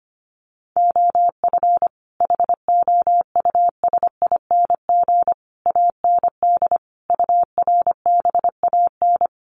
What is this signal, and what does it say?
Morse code, heard as a keyed tone, about 25 wpm, 700 Hz. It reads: OF HOUSING AND URBAN
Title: Sets of 5 Words Encore2 25wpm